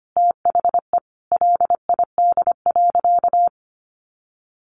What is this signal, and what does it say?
Morse code, heard as a keyed tone, about 25 wpm, 700 Hz. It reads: THE LID.
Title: Easy Phrases 25wpm